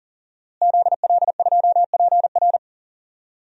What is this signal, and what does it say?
Morse code, heard as a keyed tone, about 40 wpm, 700 Hz. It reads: ZL2PR
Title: DXCC-100 Call Signs - 5 Characters - Repeated 3x - 40wpm